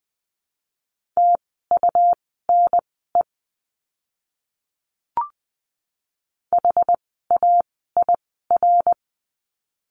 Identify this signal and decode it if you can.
Morse code, heard as a keyed tone, about 20 wpm, spaced as for 15 wpm, 700 Hz. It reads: TUNE HAIR
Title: 101 Lesson 17f - Letter u - Words 20wpm